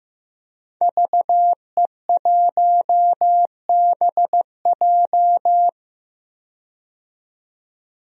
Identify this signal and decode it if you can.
Morse code, heard as a keyed tone, about 15 wpm, 700 Hz. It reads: VE1BJ
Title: DXCC-100 Call Signs - 5 Characters - 15wpm